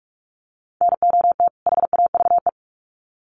Morse code, heard as a keyed tone, about 44 wpm, 700 Hz. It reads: NOT HAVE
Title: Sets of 2 Words 45wpm